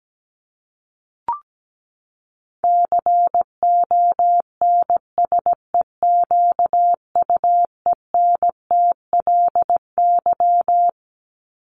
Morse code, heard as a keyed tone, about 17 wpm, 700 Hz. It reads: CONSEQUENTLY
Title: LY Suffix - Top 100 Words with Base Words 17wpm